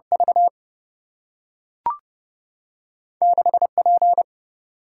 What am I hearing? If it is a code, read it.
Morse code, heard as a keyed tone, about 30 wpm, 700 Hz. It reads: V 6 P
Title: Single-Letter Single-Number 30wpm